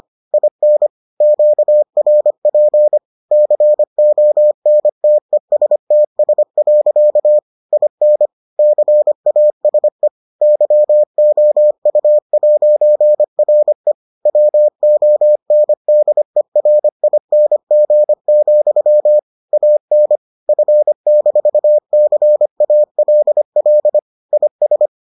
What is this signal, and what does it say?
Morse code, heard as a keyed tone, about 25 wpm, 600 Hz. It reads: IN QRP CONTESTS. IN CASE YOU'RE WONDERING, AN F-CALL IS